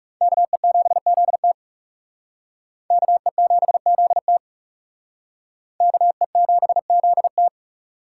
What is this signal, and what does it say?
Morse code, heard as a keyed tone, about 37 wpm, 700 Hz. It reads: KE7ZT KE7ZT KE7ZT
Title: Speed Racing - USA Call Signs 30wpm